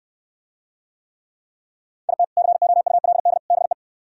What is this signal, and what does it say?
Morse code, heard as a keyed tone, about 66 wpm, 700 Hz. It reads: IT COULD BE